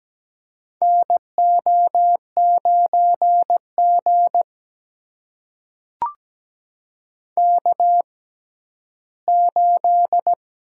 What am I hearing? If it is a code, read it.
Morse code, heard as a keyed tone, about 17 wpm, spaced as for 11 wpm, 700 Hz. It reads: NO9G K8
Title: USA Call Signs - 4 Characters - 6x Character Spacing - 17wpm